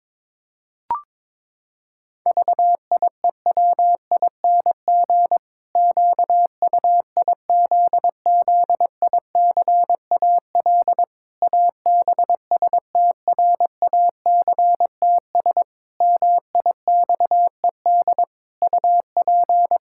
Morse code, heard as a keyed tone, about 22 wpm, 700 Hz. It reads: VIEWING QUIZZICAL ABSTRACTS MIXED UP
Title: Pangram Challenge v2 - 22wpm